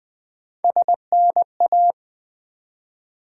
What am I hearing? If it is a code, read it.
Morse code, heard as a keyed tone, about 20 wpm, 700 Hz. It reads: SNA
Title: Build A Word - 1 Letter At A Time - 4-Letter Words - 20wpm